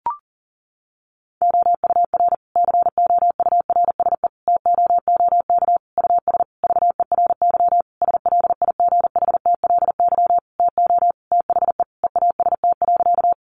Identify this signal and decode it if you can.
Morse code, heard as a keyed tone, about 40 wpm, 700 Hz. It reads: OUR COURSE TOOK US VERY SLIGHTLY TO THE EAST.
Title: Sentences from Top 1300 Words 40wpm